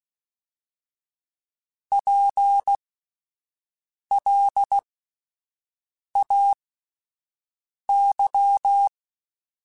Morse code, heard as a keyed tone, about 16 wpm, spaced as for 5 wpm, 800 Hz. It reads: PLAY
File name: General / General-mp3s / Houses @10